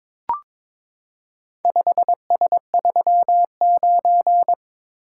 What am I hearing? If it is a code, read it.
Morse code, heard as a keyed tone, about 22 wpm, 700 Hz. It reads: 5S39
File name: practice / 22wpm / Four Letter-Number Combinations - 22wpm